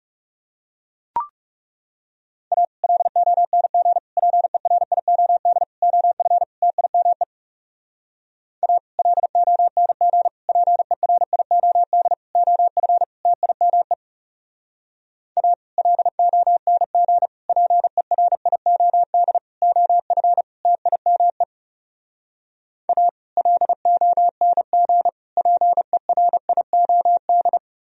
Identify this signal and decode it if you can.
Morse code, heard as a keyed tone, about 37 wpm, 700 Hz. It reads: A LONG PERIOD OF TIME A LONG PERIOD OF TIME A LONG PERIOD OF TIME A LONG PERIOD
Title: Speed-Racing - Sets of 5 Words 30wpm